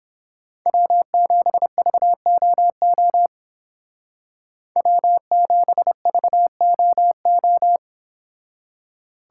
Morse code, heard as a keyed tone, about 30 wpm, 700 Hz. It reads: W7VOO W7VOO